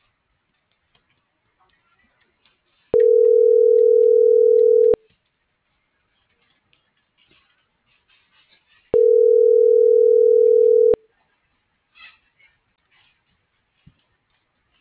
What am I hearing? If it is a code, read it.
Ringback tone (North America), 440 + 480 Hz.
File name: CENTRAL SOTOVA / out alejandro.rodriguez - .48995